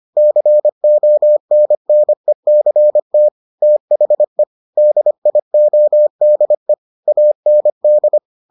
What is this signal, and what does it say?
Morse code, heard as a keyed tone, about 25 wpm, 600 Hz. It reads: CONNECT THE DIODE AND